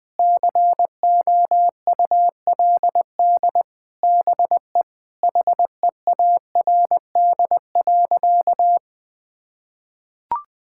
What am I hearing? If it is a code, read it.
Morse code, heard as a keyed tone, about 20 wpm, 700 Hz. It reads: COULD BE HEARD.
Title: Sentences from Top 1000 Words - Review Part 5 - 20wpm